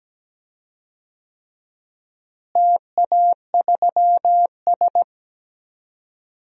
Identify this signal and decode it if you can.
Morse code, heard as a keyed tone, about 17 wpm, 700 Hz. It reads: TA3S
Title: DXCC-100 Call Signs - 4 Characters - Repeated 3x - 17wpm